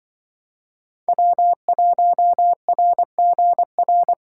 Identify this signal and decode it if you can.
Morse code, heard as a keyed tone, about 23 wpm, 700 Hz. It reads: W1RGR
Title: Speed Racing - USA 1x3 Call Signs 20wpm